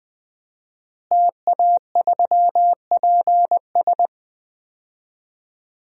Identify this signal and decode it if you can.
Morse code, heard as a keyed tone, about 20 wpm, 700 Hz. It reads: TA3PS
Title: DXCC-100 Call Signs - 5 Characters - Repeated 3x - 20wpm